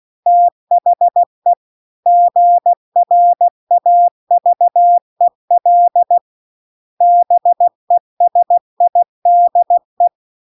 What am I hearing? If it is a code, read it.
Morse code, heard as a keyed tone, about 16 wpm, 700 Hz. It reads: THE GRAVEL BESIDE
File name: CW-The-Big-Twoheaded-River-2-16WPM